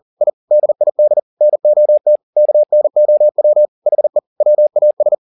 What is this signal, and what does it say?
Morse code, heard as a keyed tone, about 40 wpm, 600 Hz. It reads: I DID NOT KNOW HE WAS